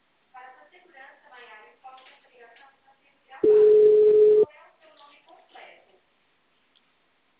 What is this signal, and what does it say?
Ringback tone (Europe, 425 Hz), with speech.